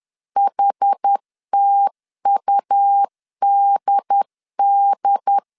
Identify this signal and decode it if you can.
Morse code, heard as a keyed tone, about 10 wpm, 800 Hz. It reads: HTUDD
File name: 10wpm